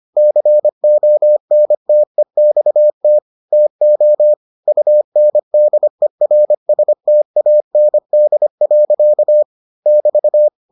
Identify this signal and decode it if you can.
Morse code, heard as a keyed tone, about 25 wpm, 600 Hz. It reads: CONTEXT TO UNDERSTAND. <BT>